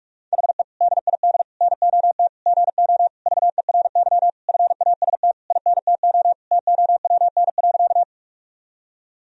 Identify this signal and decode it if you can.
Morse code, heard as a keyed tone, about 46 wpm, 700 Hz. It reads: HE DID NOT GO VERY FAST INTO TOWN.
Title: Sentences from Top 1000 Words - Review Part 2 - 45wpm